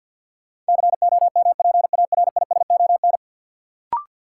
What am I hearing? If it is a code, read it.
Morse code, heard as a keyed tone, about 46 wpm, 700 Hz. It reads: COMPARISON
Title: Sentences from Top 1600 Words - New Word Drill - 50wpm